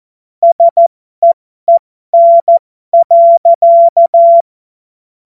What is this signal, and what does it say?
Morse code, heard as a keyed tone, about 14 wpm, spaced as for 12 wpm, 680 Hz. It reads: SEEN.
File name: Morsekurs / Witze